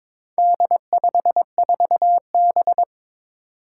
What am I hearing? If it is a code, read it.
Morse code, heard as a keyed tone, about 22 wpm, 700 Hz. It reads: D54B
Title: Four Challenging Letter-Number Combinations 22wpm